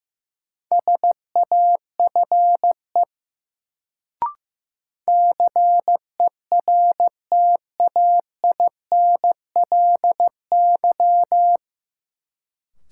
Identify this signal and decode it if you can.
Morse code, heard as a keyed tone, about 15 wpm, 700 Hz. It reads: SAFE CERTAINLY
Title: Sentences from Top 900 Words - New Word Drill - 15wpm